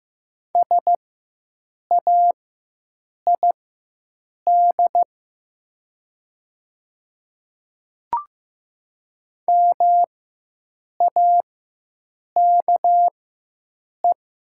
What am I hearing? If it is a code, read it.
Morse code, heard as a keyed tone, about 15 wpm, spaced as for 7 wpm, 700 Hz. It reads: SAID MAKE
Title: 4x Character Spacing - Top 100 Words - 15wpm